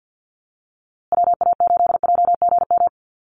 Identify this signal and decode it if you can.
Morse code, heard as a keyed tone, about 50 wpm, 700 Hz. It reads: WA8JGM
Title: USA Call Signs Repeated 2x 50wpm